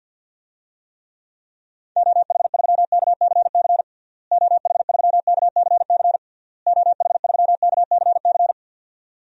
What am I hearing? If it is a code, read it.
Morse code, heard as a keyed tone, about 52 wpm, 700 Hz. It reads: OH3KCC OH3KCC OH3KCC
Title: DXCC-100 Call Signs - 6 Characters - Repeated 3x - 50wpm